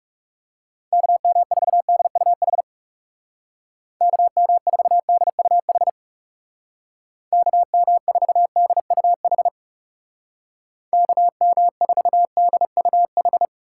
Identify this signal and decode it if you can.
Morse code, heard as a keyed tone, about 37 wpm, 700 Hz. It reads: KM4DUH KM4DUH KM4DUH KM4DUH
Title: Speed Racing - USA 2x3 Call Signs 30wpm